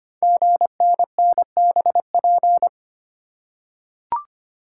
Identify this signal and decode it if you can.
Morse code, heard as a keyed tone, about 25 wpm, 700 Hz. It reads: GNNBP
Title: Code Groups - 5 Letters - 25wpm